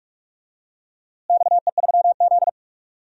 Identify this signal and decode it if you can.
Morse code, heard as a keyed tone, about 44 wpm, 700 Hz. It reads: XE3Z
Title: DXCC-100 Call Signs - 4 Characters - Repeated 2x - 45wpm